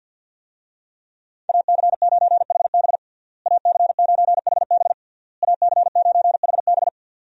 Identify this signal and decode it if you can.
Morse code, heard as a keyed tone, about 52 wpm, 700 Hz. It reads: AC9HB AC9HB AC9HB
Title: USA Call Signs - 5 Characters - Repeated 3x - 50wpm